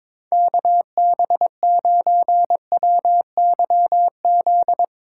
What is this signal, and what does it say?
Morse code, heard as a keyed tone, about 22 wpm, 700 Hz. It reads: KB9WYZ
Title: USA Call Signs - 6 Characters - 4x Character Spacing - 22wpm